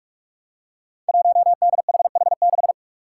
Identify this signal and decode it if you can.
Morse code, heard as a keyed tone, about 41 wpm, 700 Hz. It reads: 1DHH6
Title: Five Challenging Letter-Number Combinations 45wpm